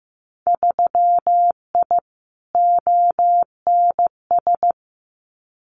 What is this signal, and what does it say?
Morse code, heard as a keyed tone, about 15 wpm, 700 Hz. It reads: 3I ONS
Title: Winter Field Day Exchanges Encore 15wpm